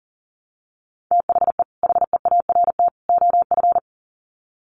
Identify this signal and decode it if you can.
Morse code, heard as a keyed tone, about 40 wpm, 700 Hz. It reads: THE HEART OF
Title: Sets of 3 Words 40wpm